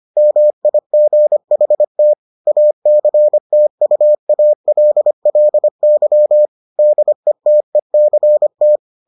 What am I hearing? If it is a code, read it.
Morse code, heard as a keyed tone, about 25 wpm, 600 Hz. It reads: MIGHT ACTUALLY DETECT